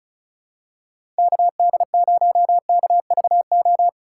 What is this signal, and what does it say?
Morse code, heard as a keyed tone, about 34 wpm, 700 Hz. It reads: KD0KVO
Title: USA Call Signs - 6 Characters - 2x Character Spacing - 35wpm